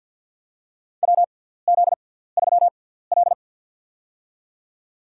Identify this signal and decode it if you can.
Morse code, heard as a keyed tone, about 49 wpm, spaced as for 17 wpm, 690 Hz. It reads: WZ3L
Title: USA Call Signs - 4 Characters - 6x Character Spacing - 50wpm